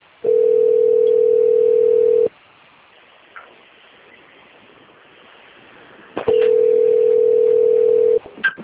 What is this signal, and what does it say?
Ringback tone (North America), 440 + 480 Hz.
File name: rg-1600-unknown- - - .23080